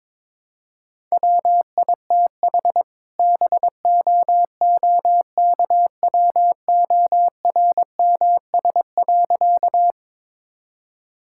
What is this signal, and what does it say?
Morse code, heard as a keyed tone, about 22 wpm, 700 Hz. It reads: WITH BOOKWORMS.
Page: morse-code-ninja-practice-22wpm